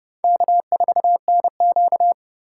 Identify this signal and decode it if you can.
Morse code, heard as a keyed tone, about 29 wpm, 700 Hz. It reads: K4NQ